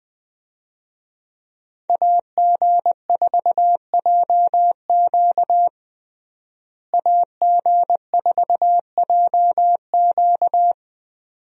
Morse code, heard as a keyed tone, about 20 wpm, 700 Hz. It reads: AG4JQ AG4JQ
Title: USA Call Signs - 5 Characters - Repeated 2x - 20wpm